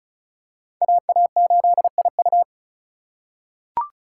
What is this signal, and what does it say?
Morse code, heard as a keyed tone, about 34 wpm, 700 Hz. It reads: AA8IU
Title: USA Call Signs - 5 Characters - 4x Character Spacing - 35wpm